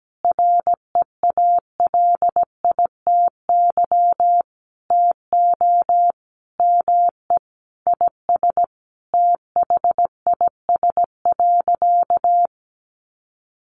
Morse code, heard as a keyed tone, about 17 wpm, 700 Hz. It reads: REALITY TO ME IS THIS.
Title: Sentences from Top 1200 Words 17wpm